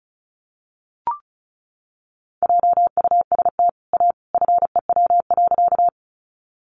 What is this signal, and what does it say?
Morse code, heard as a keyed tone, about 34 wpm, 700 Hz. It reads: JUST A FEW.
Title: Sentences from Top 300 Words 35wpm